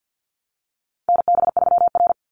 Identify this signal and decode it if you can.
Morse code, heard as a keyed tone, about 48 wpm, 700 Hz. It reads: NB3R
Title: Call Signs Repeated 2x - 50wpm